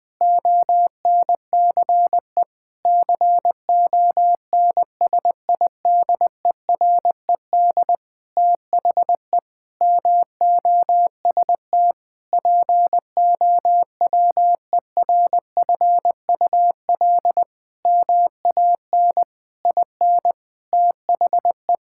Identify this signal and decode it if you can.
Morse code, heard as a keyed tone, about 21 wpm, 700 Hz. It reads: ONCE CONSIDERED THE MOST POWERFUL MAN IN THE
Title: Sentences with 7-Word Phrases - 20wpm